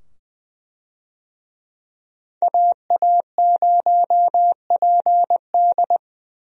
Morse code, heard as a keyed tone, about 20 wpm, 700 Hz. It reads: AA0PD
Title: USA Call Signs - 5 Characters - 20wpm